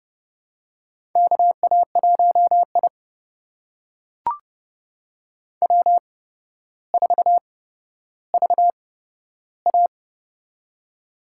Morse code, heard as a keyed tone, about 30 wpm, spaced as for 13 wpm, 700 Hz. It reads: KA1I W4VA
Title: USA Call Signs - 4 Characters - 8x Character Spacing - 30wpm